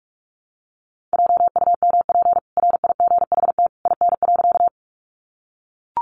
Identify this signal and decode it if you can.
Morse code, heard as a keyed tone, about 44 wpm, 700 Hz. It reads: JUMP RIGHT IN.
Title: Sentences from Top 1300 Words 45wpm